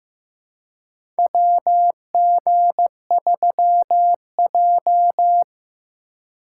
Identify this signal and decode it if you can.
Morse code, heard as a keyed tone, about 15 wpm, 700 Hz. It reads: WG3J
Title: USA Call Signs - 4 Characters - 2x Character Spacing - 15wpm